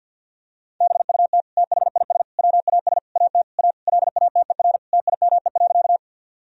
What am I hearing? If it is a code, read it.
Morse code, heard as a keyed tone, about 50 wpm, 700 Hz. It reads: BUT THIS WAS AT A LATER TIME.
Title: Sentences from Top 1000 Words - Review Part 4 - 50wpm